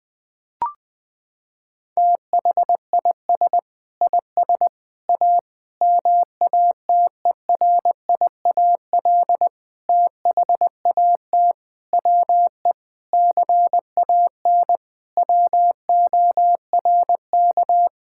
Morse code, heard as a keyed tone, about 21 wpm, 700 Hz. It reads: THIS IS A MATERIAL THAT WE CAN WORK
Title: Sentences from Top 1000 Words - Review Part 5 - 20wpm